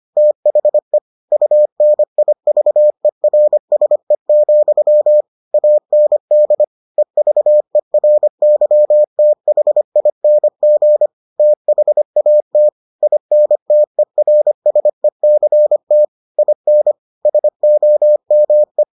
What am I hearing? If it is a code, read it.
Morse code, heard as a keyed tone, about 25 wpm, 600 Hz. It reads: THE UNIVERSE, AND EVERYTHING THAT INTERSECT IN SOME